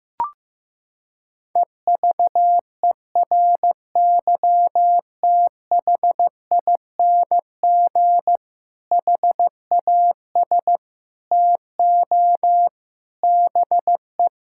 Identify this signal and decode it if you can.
Morse code, heard as a keyed tone, about 15 wpm, 700 Hz. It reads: EVERYTHING HAS TO BE
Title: Sentences from Top 1600 Words 15wpm